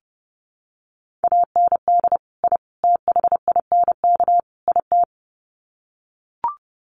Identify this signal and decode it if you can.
Morse code, heard as a keyed tone, about 30 wpm, 700 Hz. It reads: AND I THINK IT
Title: Sets of 4 Words 30wpm